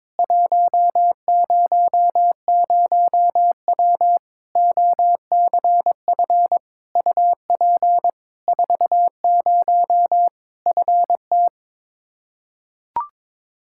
Morse code, heard as a keyed tone, about 22 wpm, 700 Hz. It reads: 100W OCF UP 40 FT
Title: K1USN SST QSO Practice 22wpm